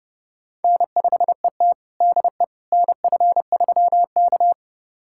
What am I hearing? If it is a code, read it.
Morse code, heard as a keyed tone, about 30 wpm, 700 Hz. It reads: N5ET DE NF3K
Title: USA Callsign DE Callsign - 4 Characters - 30wpm